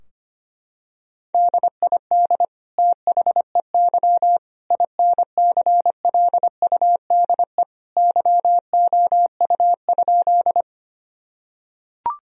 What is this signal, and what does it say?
Morse code, heard as a keyed tone, about 25 wpm, 700 Hz. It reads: DID THEY INCLUDE YOU?
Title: Sentences from Top 1000 Words - Review Part 3 - 25wpm